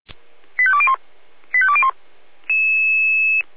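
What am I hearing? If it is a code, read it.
Five-tone selective call (ZVEI), heard as quick runs of five tones, tones of about 70 ms. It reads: 85201 85201
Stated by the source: ZVEI five-tone selective call 85340 received via SDR.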